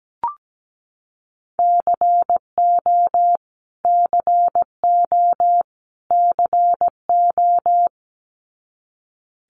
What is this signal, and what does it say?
Morse code, heard as a keyed tone, about 17 wpm, 700 Hz. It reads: CO CO CO
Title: US States Repeated 3x 17wpm